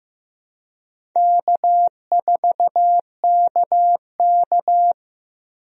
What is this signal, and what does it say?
Morse code, heard as a keyed tone, about 15 wpm, 700 Hz. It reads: K4KK
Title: USA Call Signs - 4 Characters - Repeated 3x - 15wpm